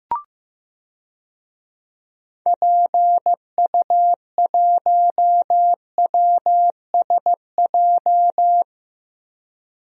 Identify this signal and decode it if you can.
Morse code, heard as a keyed tone, about 15 wpm, 700 Hz. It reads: PU1WSJ
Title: DXCC-100 Call Signs - 6 Characters - Repeated 3x - 15wpm